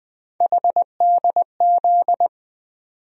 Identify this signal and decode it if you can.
Morse code, heard as a keyed tone, about 21 wpm, 700 Hz. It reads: HDZ
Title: USA Call Sign Postfixes - 3 Letters 20wpm